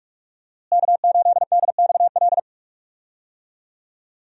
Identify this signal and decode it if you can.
Morse code, heard as a keyed tone, about 46 wpm, 700 Hz. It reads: K8DXL
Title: USA Call Signs - 5 Characters - Repeated 3x - 45wpm